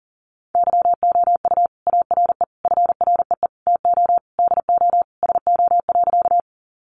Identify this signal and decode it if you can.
Morse code, heard as a keyed tone, about 40 wpm, 700 Hz. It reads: YOU ARE FREE TO DO SO.
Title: Sentences from Top 500 Words 40wpm